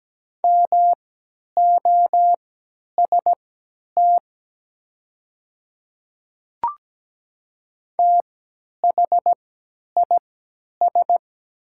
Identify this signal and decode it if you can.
Morse code, heard as a keyed tone, about 17 wpm, spaced as for 10 wpm, 700 Hz. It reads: MOST THIS
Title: 3x Character Spacing - Top 100 Words - 17wpm